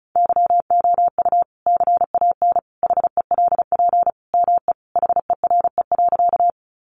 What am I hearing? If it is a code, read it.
Morse code, heard as a keyed tone, about 34 wpm, 700 Hz. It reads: YOU CAN HELP ME HERE.
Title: Sentences from Top 200 Words 35wpm